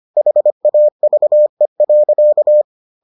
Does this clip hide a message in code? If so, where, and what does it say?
Yes: Morse code from 0.2 to 2.6 s, heard as a keyed tone, about 25 wpm, 600 Hz. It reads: HAVE.